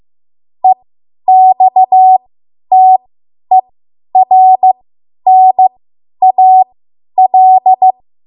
Morse code, heard as a keyed tone, about 15 wpm, spaced as for 10 wpm, 750 Hz. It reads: EXTERNAL
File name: Archive / 10 WPM / 10WPM